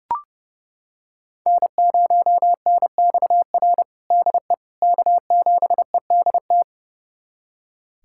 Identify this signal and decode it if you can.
Morse code, heard as a keyed tone, about 30 wpm, 700 Hz. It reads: N0NXR DE K7EDT
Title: USA Callsign DE Callsign - 5 Characters - 30wpm